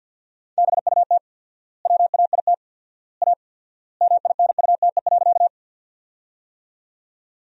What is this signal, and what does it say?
Morse code, heard as a keyed tone, about 51 wpm, 700 Hz. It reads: BUT WAIT A MINUTE.
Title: 4x Word-Spacing - Sentences from Top 500 Words - 50wpm